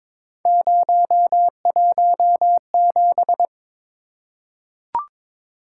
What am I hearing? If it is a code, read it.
Morse code, heard as a keyed tone, about 22 wpm, 700 Hz. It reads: 017
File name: practice / 22wpm / Three Letter-Number Combinations - 22wpm